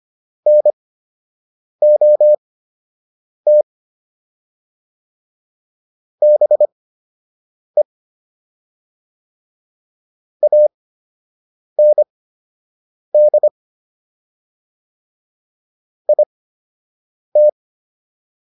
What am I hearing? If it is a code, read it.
Morse code, heard as a keyed tone, about 25 wpm, spaced as for 7 wpm, 600 Hz. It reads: NOT BE AND IT